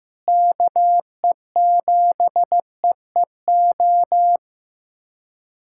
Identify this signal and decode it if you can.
Morse code, heard as a keyed tone, about 15 wpm, 700 Hz. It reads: KE7EEO
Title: USA Callsign DE Callsign 15wpm